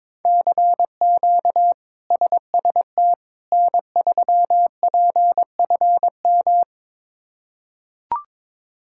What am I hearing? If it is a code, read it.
Morse code, heard as a keyed tone, about 22 wpm, 700 Hz. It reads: CQ SST N3PFM
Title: K1USN SST QSO Practice - 22wpm